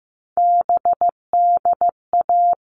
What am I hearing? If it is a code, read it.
Morse code, heard as a keyed tone, about 15 wpm, 700 Hz. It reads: BDA
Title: All Cut Numbers - 3-Digit 15wpm